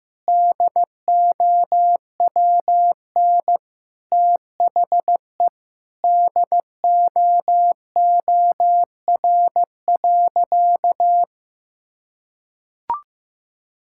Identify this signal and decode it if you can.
Morse code, heard as a keyed tone, about 15 wpm, 700 Hz. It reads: DOWN THE DOOR.
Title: Sentences from Top 1000 Words - Review Part 6 - 15wpm